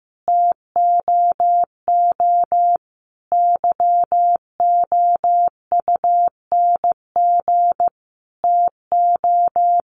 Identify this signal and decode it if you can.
Morse code, heard as a keyed tone, about 15 wpm, 700 Hz. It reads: TOO YOUNG TO
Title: Sentences from Top 600 Words 15wpm